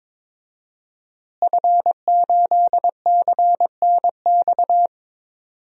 Morse code, heard as a keyed tone, about 22 wpm, 700 Hz. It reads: F8CNX
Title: DXCC-100 Call Signs - 5 Characters - Repeated 2x - 22wpm